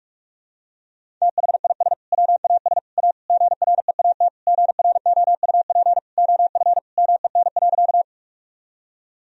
Morse code, heard as a keyed tone, about 43 wpm, 700 Hz. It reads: THIS WAS A GREAT GROUP OF MEN.
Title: Sentences from Top 1000 Words - Review Part 1 - 45wpm